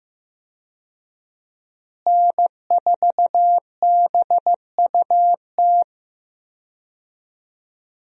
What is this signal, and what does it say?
Morse code, heard as a keyed tone, about 15 wpm, 700 Hz. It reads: N4BUT
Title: USA Call Signs - 5 Characters - Repeated 3x - 15wpm